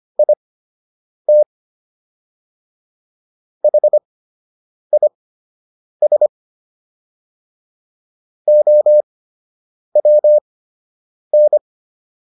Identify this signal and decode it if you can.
Morse code, heard as a keyed tone, about 25 wpm, spaced as for 8 wpm, 600 Hz. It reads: IT HIS OWN